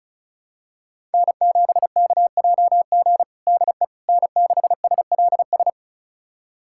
Morse code, heard as a keyed tone, about 34 wpm, 700 Hz. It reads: N7KJG DE N6SLS
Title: USA Callsign DE Callsign - 5 Characters - 35wpm